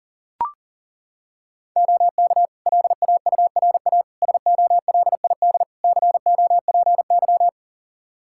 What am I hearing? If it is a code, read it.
Morse code, heard as a keyed tone, about 40 wpm, 700 Hz. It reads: OK LAURA SOLID COPY
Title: K1USN SST QSO Practice 40wpm